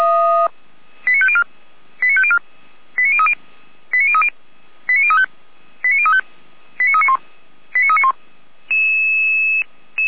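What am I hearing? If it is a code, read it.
Five-tone selective call (ZVEI), heard as quick runs of five tones, tones of about 80 ms. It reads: 89504 89504 89030 89030 89035 89035 89391 89391
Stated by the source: ZVEI five-tone selective call received with an SDR.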